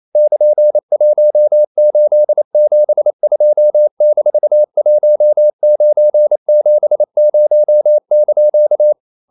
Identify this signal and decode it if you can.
Morse code, heard as a keyed tone, about 28 wpm, 600 Hz. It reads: <KN>1872-1970)
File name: CW / Practice / EnglishQuotations0128